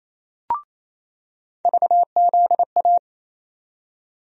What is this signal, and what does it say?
Morse code, heard as a keyed tone, about 28 wpm, 700 Hz. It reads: VZA
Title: USA Call Sign Postfixes - 3 Letters 28wpm